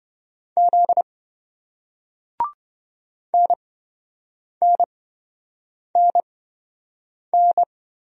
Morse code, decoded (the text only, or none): Z NNNN